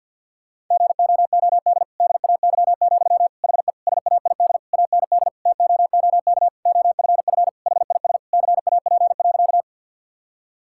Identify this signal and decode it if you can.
Morse code, heard as a keyed tone, about 50 wpm, 700 Hz. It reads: GOOD DAY, HE SAID AND TOOK OFF HIS CAP.
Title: Sentences from Top 1600 Words 50wpm